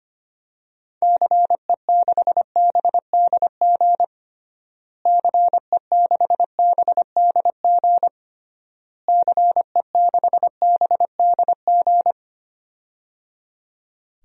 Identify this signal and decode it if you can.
Morse code, heard as a keyed tone, about 25 wpm, 700 Hz. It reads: CE6BDG CE6BDG CE6BDG
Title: DXCC-100 Call Signs - 6 Characters - Repeated 3x - 25wpm